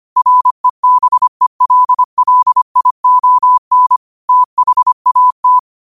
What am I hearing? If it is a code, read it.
Morse code, heard as a keyed tone, about 25 wpm, 1000 Hz. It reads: REBELLION THAT
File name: qotdfiles / 25wpm